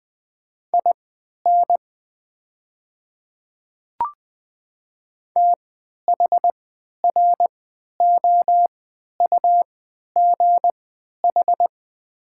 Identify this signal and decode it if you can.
Morse code, heard as a keyed tone, about 20 wpm, spaced as for 11 wpm, 700 Hz. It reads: IN THROUGH
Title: 3x Character Spacing - Top 500 Words - 20wpm